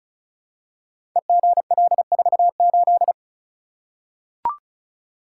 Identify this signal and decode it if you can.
Morse code, heard as a keyed tone, about 36 wpm, 700 Hz. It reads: EGL48
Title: Code Groups - 5 Letter:Numbers - 35wpm